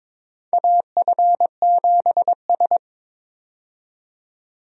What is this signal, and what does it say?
Morse code, heard as a keyed tone, about 22 wpm, 700 Hz. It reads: AF7S
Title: USA Call Signs - 4 Characters - 22wpm